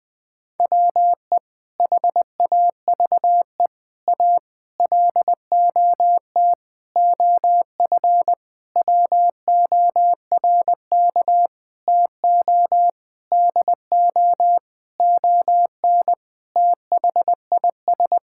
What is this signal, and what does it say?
Morse code, heard as a keyed tone, about 20 wpm, 700 Hz. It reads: WE HAVE A LOT OF WORK TO DO ON THIS
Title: Sentences with 6-Word Phrases - 20wpm